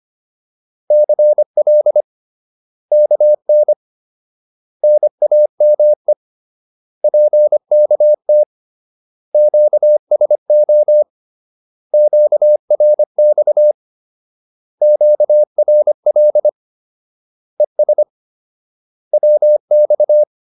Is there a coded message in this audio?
Yes: Morse code from 0.9 to 20.2 s, heard as a keyed tone, about 25 wpm, 600 Hz. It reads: CL KN NAME PKT QSO QRX QRL ES WX